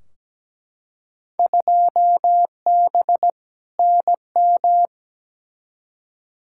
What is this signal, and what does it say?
Morse code, heard as a keyed tone, about 17 wpm, 700 Hz. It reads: 2B NM
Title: Field Day Exchanges Encore - 17wpm